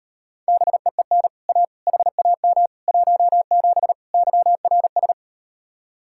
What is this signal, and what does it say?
Morse code, heard as a keyed tone, about 38 wpm, 700 Hz. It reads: BEEN A HAM 17 YRS